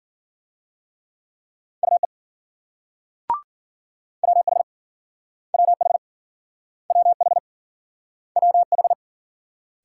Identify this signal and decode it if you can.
Morse code, heard as a keyed tone, about 53 wpm, 700 Hz. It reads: UE WH WH WH WH